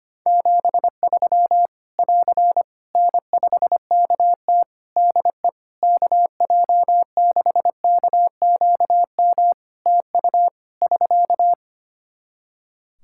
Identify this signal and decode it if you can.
Morse code, heard as a keyed tone, about 25 wpm, 700 Hz. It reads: 73 <AR> N5KT DE KJ6KQM TU <SK>
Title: Practice QSO - Exchange 4 - Ending - 25wpm